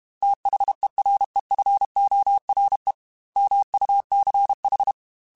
Morse code, heard as a keyed tone, about 32 wpm, 800 Hz. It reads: THEREFORE MUCH
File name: Technical Subjects / Technical Subjects-mp3s / AND Gate @30